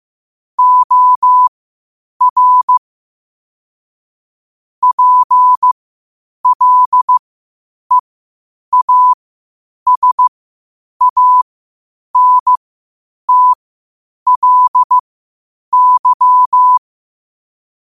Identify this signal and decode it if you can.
Morse code, heard as a keyed tone, about 15 wpm, spaced as for 8 wpm, 1000 Hz. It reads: OR PLEASANTLY
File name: qotdfiles / 8wpm